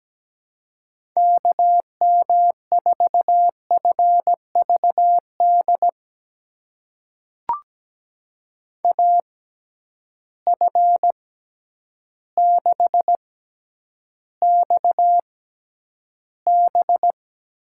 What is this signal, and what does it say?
Morse code, heard as a keyed tone, about 17 wpm, spaced as for 9 wpm, 700 Hz. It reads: KM4FVD AF6XB
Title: USA Call Signs - 6x Character Spacing - 17wpm